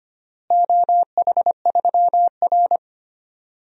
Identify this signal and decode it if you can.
Morse code, heard as a keyed tone, about 25 wpm, 700 Hz. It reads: OH3R
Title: DXCC-100 Call Signs - 4 Characters - Repeated 2x - 25wpm